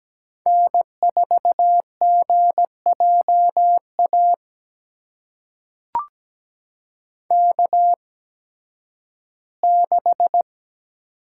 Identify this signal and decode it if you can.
Morse code, heard as a keyed tone, about 17 wpm, spaced as for 11 wpm, 700 Hz. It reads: N4GJA K6
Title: USA Call Signs - 5 Characters - 8x Character Spacing - 17wpm